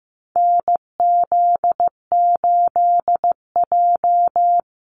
Morse code, heard as a keyed tone, about 15 wpm, 700 Hz. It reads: NZ8J
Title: Call Signs Repeated 2x - 15wpm